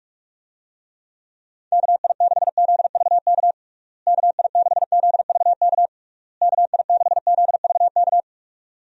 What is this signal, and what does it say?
Morse code, heard as a keyed tone, about 46 wpm, 700 Hz. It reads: KI6ZVK KI6ZVK KI6ZVK
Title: USA Call Signs - 6 Characters - Repeated 3x - 45wpm